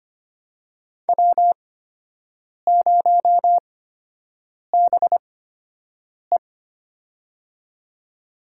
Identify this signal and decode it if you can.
Morse code, heard as a keyed tone, about 25 wpm, spaced as for 7 wpm, 700 Hz. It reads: W0BE
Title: USA Call Signs - 4 Characters - 8x Character Spacing - 25wpm